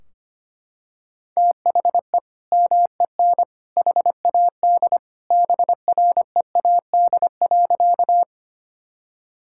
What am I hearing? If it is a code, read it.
Morse code, heard as a keyed tone, about 25 wpm, 700 Hz. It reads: THE MEN HAD BREAD.
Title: Sentences from Top 1600 Words 25wpm